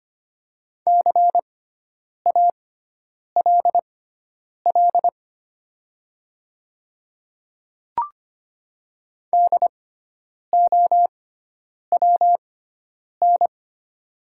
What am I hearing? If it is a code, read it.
Morse code, heard as a keyed tone, about 25 wpm, spaced as for 9 wpm, 700 Hz. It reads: CALL DOWN
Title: 6x Character Spacing - Top 100 Words - 25wpm